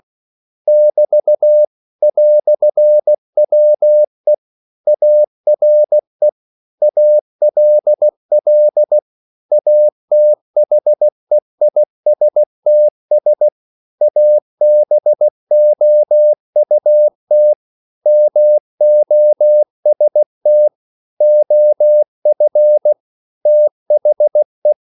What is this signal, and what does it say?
Morse code, heard as a keyed tone, about 16 wpm, 600 Hz. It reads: <BT> "WE ARE ALL ATHEISTS ABOUT MOST OF THE
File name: CW / Practice / EnglishQuotations0116